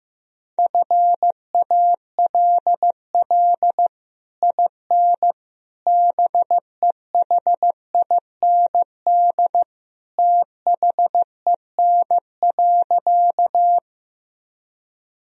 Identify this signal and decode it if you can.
Morse code, heard as a keyed tone, about 15 wpm, 700 Hz. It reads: FALL IN BEHIND THEN.
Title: Sentences from Top 1000 Words - Review Part 2 - 15wpm